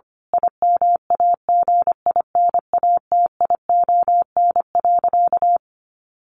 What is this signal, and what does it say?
Morse code, heard as a keyed tone, about 25 wpm, 700 Hz. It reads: IMAGINATION.
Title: Sentences from Top 1300 Words 25wpm